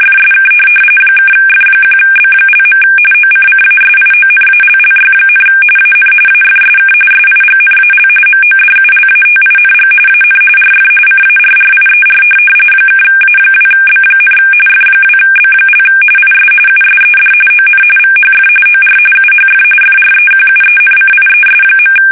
RTTY, about 45.45 baud. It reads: FSK (F1B). OCCASIONALLY, AN AFSK SIGNAL MODULATING AN RF CARRIER (A2B, F2B) IS USED ON VHF OR UHF FREQUENCIES.
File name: fsk850 / teletype-RTTY